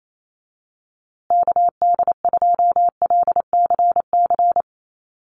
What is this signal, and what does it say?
Morse code, heard as a keyed tone, about 28 wpm, 700 Hz. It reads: KD2LCC
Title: USA Call Signs Repeated 2x 28wpm